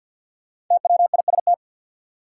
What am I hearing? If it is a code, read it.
Morse code, heard as a keyed tone, about 51 wpm, 700 Hz. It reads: TWIST
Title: Five Letter Words - 50wpm